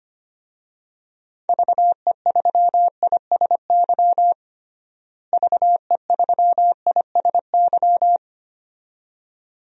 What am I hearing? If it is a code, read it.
Morse code, heard as a keyed tone, about 25 wpm, 700 Hz. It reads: VE3ISY VE3ISY
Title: DXCC-100 Call Signs - 6 Characters - Repeated 2x - 25wpm